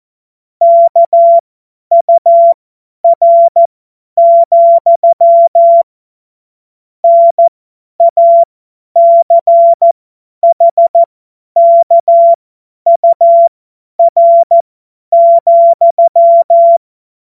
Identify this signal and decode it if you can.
Morse code, heard as a keyed tone, about 14 wpm, spaced as for 10 wpm, 680 Hz. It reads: KUR, NACHKUR,
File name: Witz-12-August-2025-50-F70